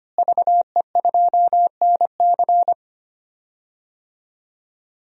Morse code, heard as a keyed tone, about 25 wpm, 700 Hz. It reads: VE2NC
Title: DXCC-100 Call Signs - 5 Characters - Repeated 2x - 25wpm